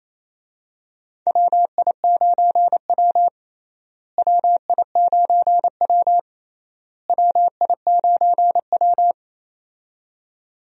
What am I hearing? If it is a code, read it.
Morse code, heard as a keyed tone, about 28 wpm, 700 Hz. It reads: WI9W WI9W WI9W
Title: USA Call Signs - 4 Characters - Repeated 3x - 28wpm